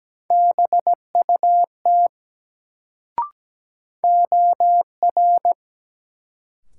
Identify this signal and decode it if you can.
Morse code, heard as a keyed tone, about 17 wpm, 700 Hz. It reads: BUT OR
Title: Sets of 2 Words - New Word Drill - 17wpm